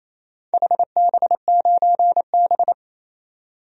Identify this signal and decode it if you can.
Morse code, heard as a keyed tone, about 28 wpm, 700 Hz. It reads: HB9B
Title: DXCC-100 Call Signs - 4 Characters - 28wpm